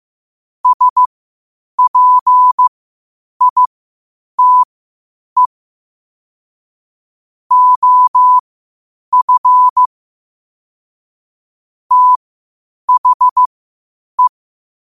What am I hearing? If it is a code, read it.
Morse code, heard as a keyed tone, about 15 wpm, spaced as for 8 wpm, 1000 Hz. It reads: SPITE OF THE
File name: qotdfiles / 8wpm